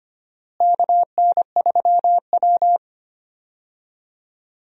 Morse code, heard as a keyed tone, about 25 wpm, 700 Hz. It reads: KN3W
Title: USA Call Signs - 4 Characters - Repeated 2x - 25wpm